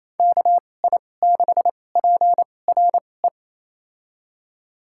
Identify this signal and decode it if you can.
Morse code, heard as a keyed tone, about 28 wpm, spaced as for 20 wpm, 700 Hz. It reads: KI6PRE
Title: USA Call Signs - 2x Character Spacing - 28wpm